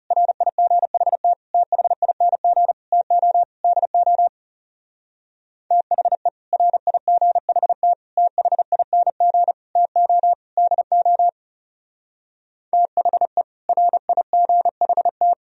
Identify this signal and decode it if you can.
Morse code, heard as a keyed tone, about 37 wpm, 700 Hz. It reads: RIGHT THING TO DO THE RIGHT THING TO DO THE RIGHT
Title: Speed-Racing - Sets of 5 Words 30wpm